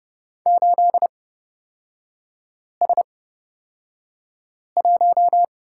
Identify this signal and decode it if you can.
Morse code, heard as a keyed tone, about 29 wpm, spaced as for 5 wpm, 700 Hz. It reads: 8S1
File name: Lesson 38e - Letter q - Three Characters 30wpm